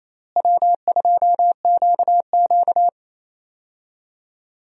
Morse code, heard as a keyed tone, about 28 wpm, 700 Hz. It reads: W2QQ
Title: USA Call Signs - 4 Characters - Repeated 3x - 28wpm